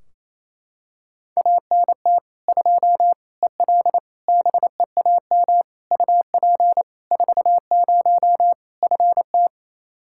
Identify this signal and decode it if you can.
Morse code, heard as a keyed tone, about 28 wpm, 700 Hz. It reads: ANT 2 EL BEAM UP 40 FT
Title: Practice QSO - Exchange 2 - Info - 28wpm